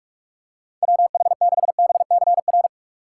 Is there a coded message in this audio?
Yes: Morse code from 0.8 to 2.7 s, heard as a keyed tone, about 44 wpm, 700 Hz. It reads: WH6BCR